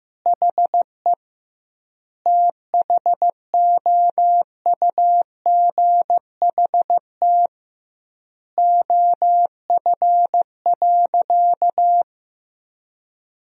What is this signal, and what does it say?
Morse code, heard as a keyed tone, about 15 wpm, 700 Hz. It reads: HE THOUGHT OF.
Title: 2x Word-Spacing - Sentences from Top 300 Words - 15wpm